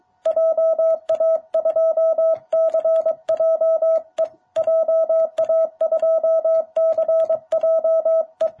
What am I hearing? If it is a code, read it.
Morse code, heard as a keyed tone, about 23 wpm, 660 Hz, with radio noise. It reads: JA2CJE JA2CJE
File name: jh8cbh / 2021cw